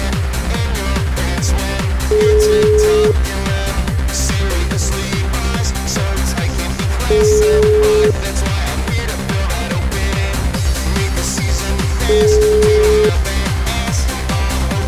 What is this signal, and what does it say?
Ringback tone (Europe, 425 Hz), with noise.